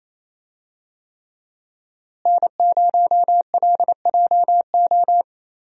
Morse code, heard as a keyed tone, about 28 wpm, 700 Hz. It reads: N0LJO
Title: USA Call Signs - 5 Characters - Repeated 3x - 28wpm